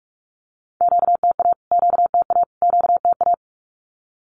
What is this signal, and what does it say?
Morse code, heard as a keyed tone, about 45 wpm, 700 Hz. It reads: QTA QTA QTA
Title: Q-Codes Repeated 3x 45wpm